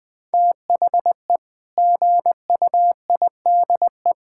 Morse code, heard as a keyed tone, about 20 wpm, 700 Hz. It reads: THE GUIDE
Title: Sentences with 8-Word Phrases - Spoken Beginning - 20wpm